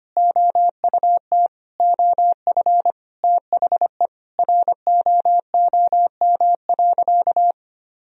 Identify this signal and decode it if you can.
Morse code, heard as a keyed tone, about 25 wpm, 700 Hz. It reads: OUT OF THE ROOM.
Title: Sentences from Top 500 Words and Abbreviations 25wpm